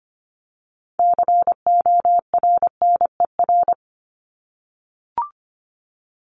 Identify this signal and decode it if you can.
Morse code, heard as a keyed tone, about 25 wpm, 700 Hz. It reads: CORNER
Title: Top 1501-1750 Words 25wpm